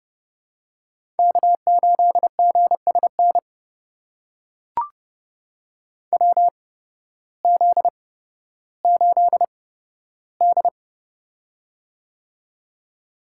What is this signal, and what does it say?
Morse code, heard as a keyed tone, about 30 wpm, spaced as for 14 wpm, 700 Hz. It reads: K8GSN WZ8D
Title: USA Call Signs - 8x Character Spacing - 30wpm